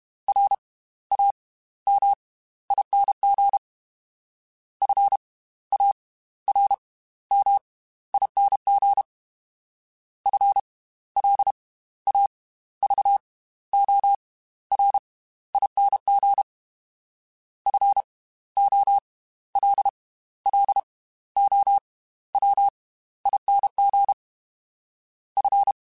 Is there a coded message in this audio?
Yes: Morse code from 0.3 to 25.7 s, heard as a keyed tone, about 32 wpm, spaced as for 15 wpm, 800 Hz. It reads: RAMING FARMING FLAVORING FOLLOWING F